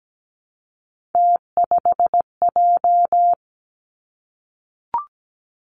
Morse code, heard as a keyed tone, about 17 wpm, 700 Hz. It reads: T5J
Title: Three Challenging Letter-Number Combinations 17wpm